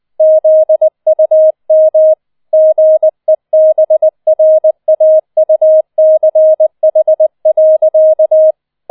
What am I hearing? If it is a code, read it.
Morse code, heard as a keyed tone, about 20 wpm, 630 Hz. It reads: ZUM GEBRAUCH.